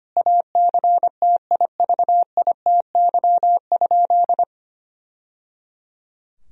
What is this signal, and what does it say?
Morse code, heard as a keyed tone, about 25 wpm, 700 Hz. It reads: ACTIVITY?
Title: Sentences from Top 1000 Words - Review Part 4 - 25wpm